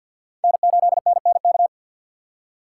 Morse code, heard as a keyed tone, about 48 wpm, 700 Hz. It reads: N8NNK
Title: USA Call Signs - 3x Character Spacing - 50wpm